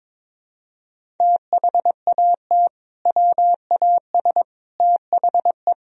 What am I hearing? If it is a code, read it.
Morse code, heard as a keyed tone, about 22 wpm, 700 Hz. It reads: THAT WAS THE
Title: Sentences from Top 400 Words and Abbreviations 22wpm